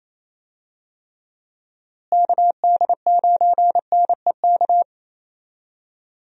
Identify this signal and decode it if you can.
Morse code, heard as a keyed tone, about 28 wpm, 700 Hz. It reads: KD9NEK
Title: USA Call Signs - 6 Characters - Repeated 3x - 28wpm